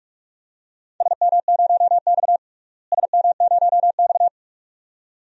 Morse code, heard as a keyed tone, about 46 wpm, 700 Hz. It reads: SM0X SM0X
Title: DXCC-100 Call Signs - 4 Characters - Repeated 2x - 45wpm